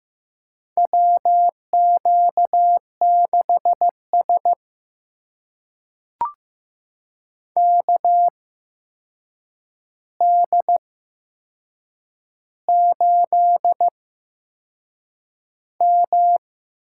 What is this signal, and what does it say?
Morse code, heard as a keyed tone, about 15 wpm, spaced as for 6 wpm, 700 Hz. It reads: WQ6S KD8M
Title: USA Call Signs - 4 Characters - 8x Character Spacing - 15wpm